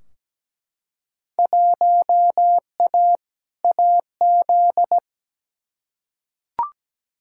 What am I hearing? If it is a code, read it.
Morse code, heard as a keyed tone, about 17 wpm, 700 Hz. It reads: 1A AZ
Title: Field Day Exchanges - 17wpm